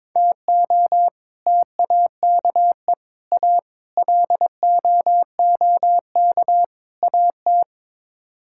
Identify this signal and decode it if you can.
Morse code, heard as a keyed tone, about 22 wpm, 700 Hz. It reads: TO TAKE A LOOK AT
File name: Build A Phrase - 5-Word Phrases - 22wpm